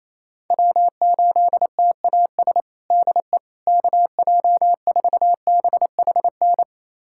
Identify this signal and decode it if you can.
Morse code, heard as a keyed tone, about 28 wpm, 700 Hz. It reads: W8TAS DE KJ4BHN
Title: USA Callsign DE Callsign 28wpm